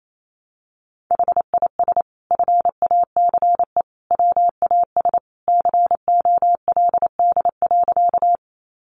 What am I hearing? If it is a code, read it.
Morse code, heard as a keyed tone, about 28 wpm, 700 Hz. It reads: HIS FACE WAS COLD.
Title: A Sound Of Thunder 28wpm